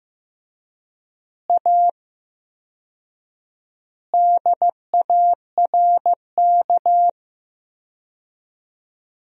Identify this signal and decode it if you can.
Morse code, heard as a keyed tone, about 15 wpm, 700 Hz. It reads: A DARK
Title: 4x Word-Spacing - Sentences from Top 500 Words - 15wpm